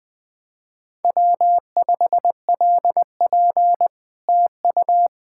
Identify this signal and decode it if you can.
Morse code, heard as a keyed tone, about 20 wpm, 700 Hz. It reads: W5LP TU
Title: Speed-Racing Common QSO Phrases Encore - 20wpm